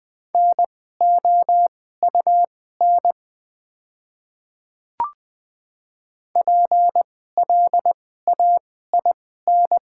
Morse code, heard as a keyed tone, about 20 wpm, spaced as for 14 wpm, 690 Hz. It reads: NOUN PLAIN
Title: 2x Character Spacing - Top 500 Words - 20wpm